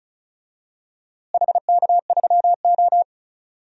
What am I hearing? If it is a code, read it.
Morse code, heard as a keyed tone, about 36 wpm, 700 Hz. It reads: HK3O
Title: DXCC-100 Call Signs - 4 Characters - Repeated 2x - 35wpm